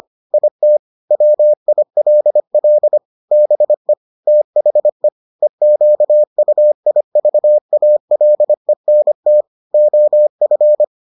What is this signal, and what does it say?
Morse code, heard as a keyed tone, about 25 wpm, 600 Hz. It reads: IT WILL BE THE EQUIVALENT OF